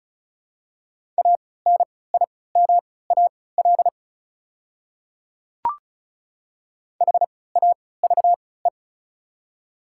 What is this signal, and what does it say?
Morse code, heard as a keyed tone, about 34 wpm, spaced as for 20 wpm, 700 Hz. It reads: ANIMAL HAVE
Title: 3x Character Spacing - Top 300 Words - 35wpm